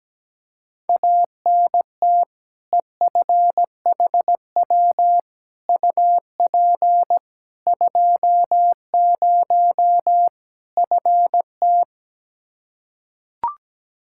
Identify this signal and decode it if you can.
Morse code, heard as a keyed tone, about 17 wpm, 700 Hz. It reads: ANT EFHW UP 20 FT
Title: Practice QSO - Exchange 2 - Info - 17wpm